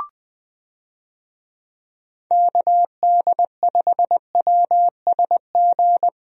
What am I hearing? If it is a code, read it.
Morse code, heard as a keyed tone, about 21 wpm, 700 Hz. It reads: KD5WSG
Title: USA Call Signs - 6 Characters - Repeated 3x - 20wpm